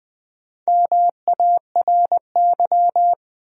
Morse code, heard as a keyed tone, about 20 wpm, 700 Hz. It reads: MARY
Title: USA Common Names - Max 4 Letters - 20wpm